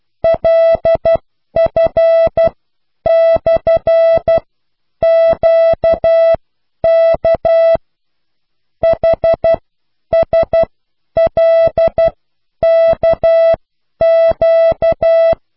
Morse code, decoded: LF/QK HSLKQ